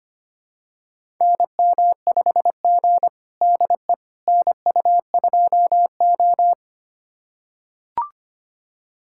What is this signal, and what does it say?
Morse code, heard as a keyed tone, about 25 wpm, 700 Hz. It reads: NM5G DE NU2O
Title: USA Callsign DE Callsign - 4 Characters - 25wpm